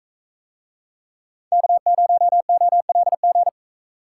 Morse code, heard as a keyed tone, about 40 wpm, 700 Hz. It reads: K0OLG